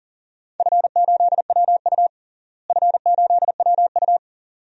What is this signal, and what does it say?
Morse code, heard as a keyed tone, about 40 wpm, 700 Hz. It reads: F8WU F8WU
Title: DXCC-100 Call Signs - 4 Characters - Repeated 2x - 40wpm